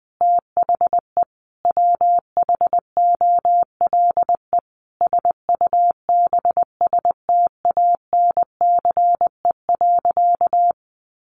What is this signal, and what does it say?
Morse code, heard as a keyed tone, about 20 wpm, 700 Hz. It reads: THE WHOLE SUBSTANCE.
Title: Sentences from Top 1500 Words 20wpm